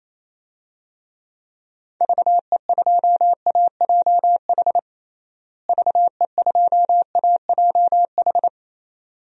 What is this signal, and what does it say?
Morse code, heard as a keyed tone, about 28 wpm, 700 Hz. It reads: VE2AJH VE2AJH
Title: DXCC-100 Call Signs - 6 Characters - Repeated 2x - 28wpm